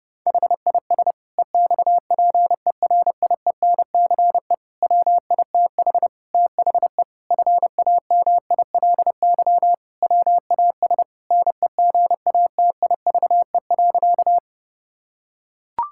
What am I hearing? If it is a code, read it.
Morse code, heard as a keyed tone, about 30 wpm, 700 Hz. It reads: HIS EXPERIENCE WITH THE FAMILY WAS NEGATIVE.
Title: Sentences from Top 1000 Words - Review Part 6 - 30wpm